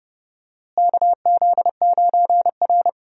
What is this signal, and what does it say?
Morse code, heard as a keyed tone, about 30 wpm, 700 Hz. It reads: KZ9R
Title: USA Call Signs - 4 Characters - Repeated 3x - 30wpm